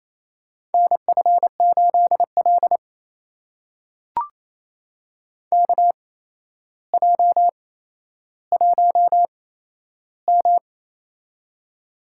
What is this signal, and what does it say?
Morse code, heard as a keyed tone, about 28 wpm, spaced as for 12 wpm, 700 Hz. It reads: NF8L KJ1M
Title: USA Call Signs - 4 Characters - 8x Character Spacing - 28wpm